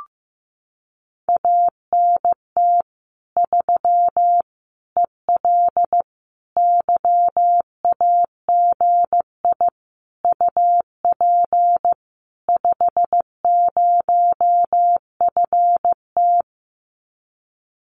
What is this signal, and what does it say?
Morse code, heard as a keyed tone, about 15 wpm, 700 Hz. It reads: ANT 3 EL YAGI UP 50FT
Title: Common QSO Phrases Encore 15wpm